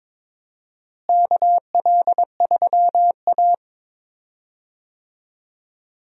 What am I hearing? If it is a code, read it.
Morse code, heard as a keyed tone, about 22 wpm, 700 Hz. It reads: KL3A
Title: USA Call Signs - 4 Characters - Repeated 2x - 22wpm